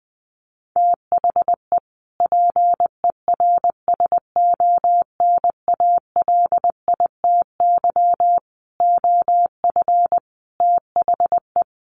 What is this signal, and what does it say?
Morse code, heard as a keyed tone, about 20 wpm, 700 Hz. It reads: THE PERSONALITY OF THE
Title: Sentences from Top 1400 Words 20wpm